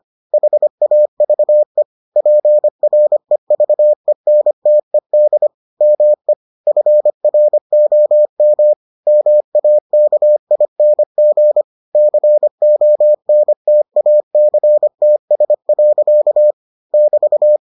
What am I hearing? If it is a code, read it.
Morse code, heard as a keyed tone, about 25 wpm, 600 Hz. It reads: HAVE PREVENTED ME FROM MAKING CONTACTS. <BT>